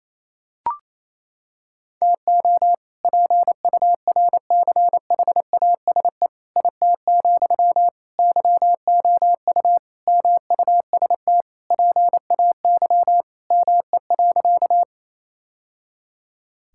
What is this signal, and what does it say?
Morse code, heard as a keyed tone, about 28 wpm, 700 Hz. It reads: TO PURCHASE IT, YOU MUST PAY ME.
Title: Sentences from Top 1000 Words - Review Part 5 - 28wpm